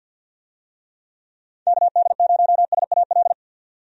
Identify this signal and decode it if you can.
Morse code, heard as a keyed tone, about 50 wpm, 700 Hz. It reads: KD0SAL
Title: USA Call Signs - 6 Characters - 50wpm